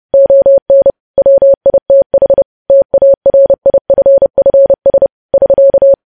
Morse code, heard as a keyed tone, about 30 wpm, 550 Hz. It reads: ON WITH TARIFFS <SK>
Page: news headlines in morse code at 30 wpm